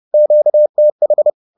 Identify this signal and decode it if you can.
Morse code, heard as a keyed tone, about 30 wpm, 600 Hz. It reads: QTH